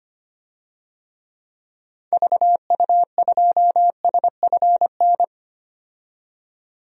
Morse code, heard as a keyed tone, about 25 wpm, 700 Hz. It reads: VU2SFN
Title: DXCC-100 Call Signs - 6 Characters - 25wpm